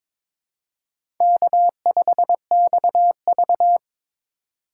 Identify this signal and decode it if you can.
Morse code, heard as a keyed tone, about 22 wpm, 700 Hz. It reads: K5XV
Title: USA Call Signs - 4 Characters - 4x Character Spacing - 22wpm